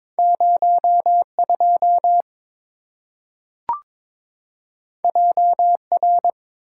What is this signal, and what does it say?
Morse code, heard as a keyed tone, about 22 wpm, 690 Hz. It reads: 02 JR
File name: practice / 22wpm / Two Letter-Number Combinations - 22wpm